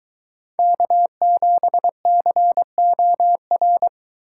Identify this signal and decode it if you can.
Morse code, heard as a keyed tone, about 23 wpm, 700 Hz. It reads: K7COR